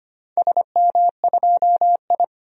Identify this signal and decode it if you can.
Morse code, heard as a keyed tone, about 25 wpm, 700 Hz. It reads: SM2I
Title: DXCC-100 Call Signs - 4 Characters - 25wpm